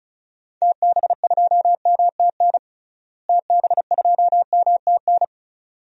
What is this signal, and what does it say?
Morse code, heard as a keyed tone, about 34 wpm, 700 Hz. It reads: TB2MTN TB2MTN
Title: DXCC-100 Call Signs - 6 Characters - Repeated 2x - 35wpm